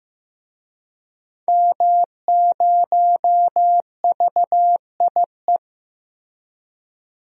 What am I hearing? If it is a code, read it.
Morse code, heard as a keyed tone, about 15 wpm, 700 Hz. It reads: M0VIE
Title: DXCC-100 Call Signs - 5 Characters - Repeated 2x - 15wpm